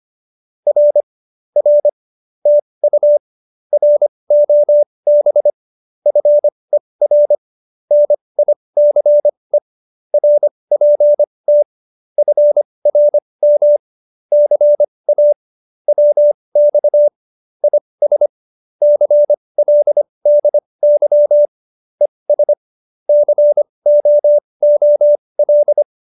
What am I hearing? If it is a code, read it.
Morse code, heard as a keyed tone, about 25 wpm, spaced as for 20 wpm, 590 Hz. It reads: R R TU ROB FER NICE RPT FRM CA WX IS CLDY ES COOL
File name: QSO301 20